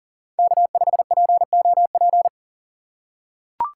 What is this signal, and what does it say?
Morse code, heard as a keyed tone, about 37 wpm, 700 Hz. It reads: K5POP
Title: USA Call Signs - 5 Characters - 6x Character Spacing - 40wpm